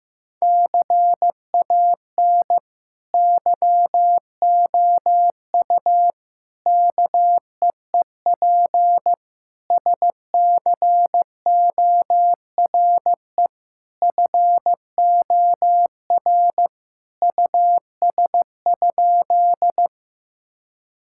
Morse code, heard as a keyed tone, about 16 wpm, 700 Hz. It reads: CAN YOU KEEP SCORE FOR US?
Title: Sentences from Top 1600 Words 15wpm